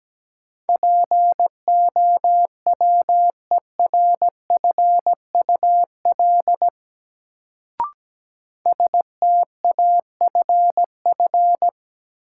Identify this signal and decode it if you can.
Morse code, heard as a keyed tone, about 17 wpm, 700 Hz. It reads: POWERFUL STAFF
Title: Sets of 5 Words Encore - New Word Drill - 17wpm